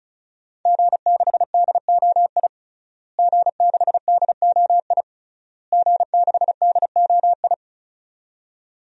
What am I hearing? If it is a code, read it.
Morse code, heard as a keyed tone, about 34 wpm, 700 Hz. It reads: G6DOI G6DOI G6DOI
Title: DXCC-100 Call Signs - 5 Characters - Repeated 3x - 35wpm